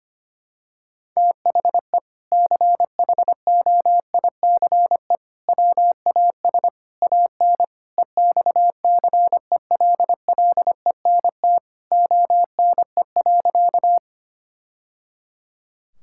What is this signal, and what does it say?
Morse code, heard as a keyed tone, about 25 wpm, 700 Hz. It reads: THE CHOICE WAS AN EXCELLENT ONE.
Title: Sentences from Top 1000 Words - Review Part 6 - 25wpm